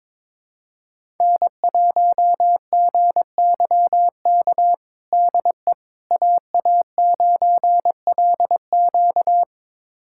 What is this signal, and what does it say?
Morse code, heard as a keyed tone, about 21 wpm, 700 Hz. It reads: N1GYK DE AA9LQ
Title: USA Callsign DE Callsign - 5 Characters - 22wpm